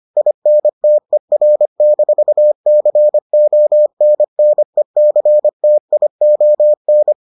Morse code, heard as a keyed tone, about 25 wpm, 600 Hz. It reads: INTER-CONNECTION